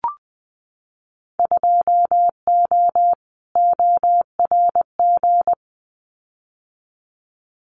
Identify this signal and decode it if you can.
Morse code, heard as a keyed tone, about 20 wpm, 700 Hz. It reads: 2O ORG
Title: Winter Field Day Exchanges 20wpm